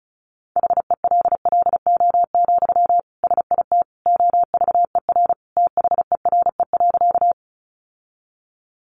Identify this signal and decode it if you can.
Morse code, heard as a keyed tone, about 34 wpm, 700 Hz. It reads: HELLO, SIT OVER THERE.
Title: Sentences from Top 1100 Words 35wpm